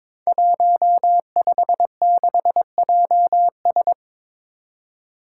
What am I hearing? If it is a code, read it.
Morse code, heard as a keyed tone, about 23 wpm, 700 Hz. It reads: 156JS
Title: Five Challenging Letter-Number Combinations 22wpm